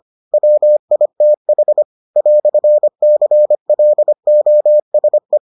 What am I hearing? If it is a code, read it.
Morse code, heard as a keyed tone, about 25 wpm, 600 Hz. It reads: WITH "CLOSE